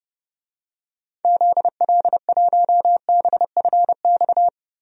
Morse code, heard as a keyed tone, about 30 wpm, 700 Hz. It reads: ZL1BFX
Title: DXCC-100 Call Signs - 6 Characters - 30wpm